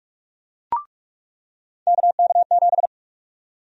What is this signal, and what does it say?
Morse code, heard as a keyed tone, about 44 wpm, 700 Hz. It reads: KK7
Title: USA Call Sign Prefixes - 45wpm